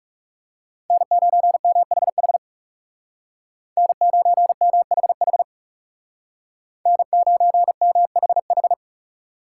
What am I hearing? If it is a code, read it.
Morse code, heard as a keyed tone, about 36 wpm, 700 Hz. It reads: N9MHH N9MHH N9MHH
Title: Speed Racing - USA 1x3 Call Signs 30wpm